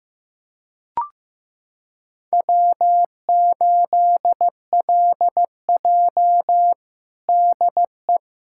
Morse code, heard as a keyed tone, about 15 wpm, 700 Hz. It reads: W8LJ DE
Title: USA Callsign DE Callsign - 4 Characters - 15wpm